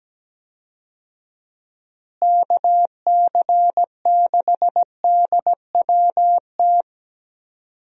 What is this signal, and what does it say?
Morse code, heard as a keyed tone, about 17 wpm, 700 Hz. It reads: KC6DWT
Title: USA Call Signs - 6 Characters - Repeated 2x - 17wpm